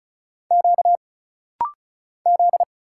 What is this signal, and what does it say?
Morse code, heard as a keyed tone, about 36 wpm, 700 Hz. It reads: Q Z
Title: Single Letter-Number - Rapid-Fire 35wpm